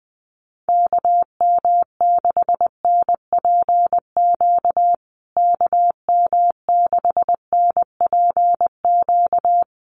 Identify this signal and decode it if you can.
Morse code, heard as a keyed tone, about 20 wpm, 700 Hz. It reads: KM6NPQ KM6NPQ
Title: USA Call Signs Repeated 3x 20wpm